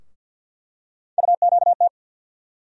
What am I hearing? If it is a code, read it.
Morse code, heard as a keyed tone, about 47 wpm, 690 Hz. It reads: UQT
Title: USA Call Sign Postfixes - 3 Letters 50wpm